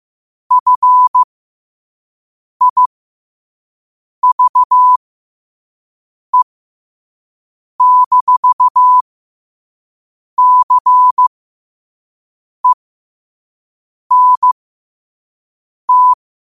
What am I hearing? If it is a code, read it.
Morse code, heard as a keyed tone, about 15 wpm, spaced as for 5 wpm, 1000 Hz. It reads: FIVE-CENT
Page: New quotes every day in morse code at 5 Words per minute.